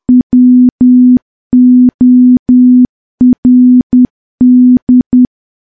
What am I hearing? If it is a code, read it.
Morse code, heard as a keyed tone, about 10 wpm, 250 Hz. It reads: WORD
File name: morse / morse-training-10WPM